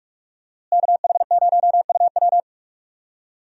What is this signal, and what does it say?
Morse code, heard as a keyed tone, about 41 wpm, 700 Hz. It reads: KH0UW
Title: USA Call Signs - 5 Characters - Repeated 3x - 45wpm